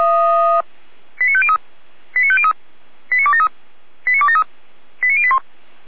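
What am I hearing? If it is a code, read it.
Five-tone selective call (ZVEI), heard as quick runs of five tones, tones of about 70 ms. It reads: 89503 89503 89273 89273 89081
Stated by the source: ZVEI five-tone selective call 89010 received by SDR, logged by Feuerwehr Orlamünde fire brigade.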